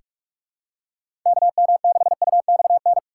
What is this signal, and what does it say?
Morse code, heard as a keyed tone, about 43 wpm, 700 Hz. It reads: KM6UXN
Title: USA Call Signs - 6 Characters - 3x Character Spacing - 45wpm